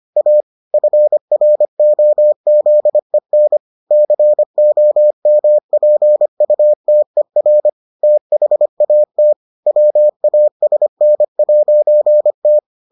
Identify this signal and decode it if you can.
Morse code, heard as a keyed tone, about 25 wpm, 600 Hz. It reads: A FROZEN COMPUTER THAT WASN'T